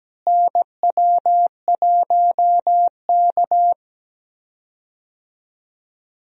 Morse code, heard as a keyed tone, about 17 wpm, 700 Hz. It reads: NW1K
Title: USA Call Signs - 4 Characters - Repeated 3x - 17wpm